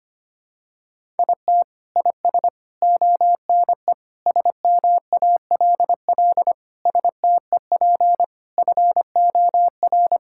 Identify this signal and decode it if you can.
Morse code, heard as a keyed tone, about 25 wpm, 700 Hz. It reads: IT IS ONE SMALL STEP FOR
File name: practice / 25wpm / Sentences with 9-Word Phrases - 25wpm